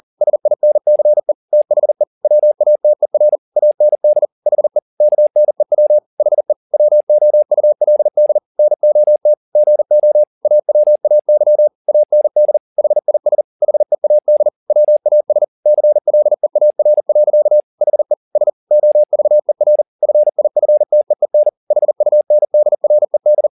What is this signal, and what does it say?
Morse code, heard as a keyed tone, about 40 wpm, 600 Hz. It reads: SINCE THE WATER AND HE KNEW HE WOULD NOT GO AWAY AND HIS HEAD WAS CLEAR. HE S OVER FIFTEEN HUNDRED
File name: the old man and the sea 2 40wpm